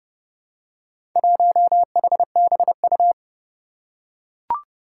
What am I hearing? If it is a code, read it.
Morse code, heard as a keyed tone, about 30 wpm, 700 Hz. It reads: 1HBU
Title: Four Challenging Letter-Number Combinations 30wpm